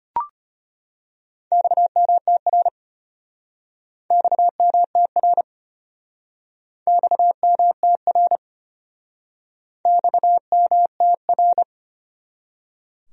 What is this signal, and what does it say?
Morse code, heard as a keyed tone, about 33 wpm, 700 Hz. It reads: XMTR XMTR XMTR XMTR